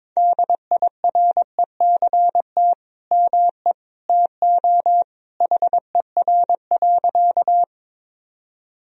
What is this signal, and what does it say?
Morse code, heard as a keyed tone, about 22 wpm, 700 Hz. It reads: DIRECT ME TO HER.
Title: Sentences from Top 1000 Words - Review Part 2 - 22wpm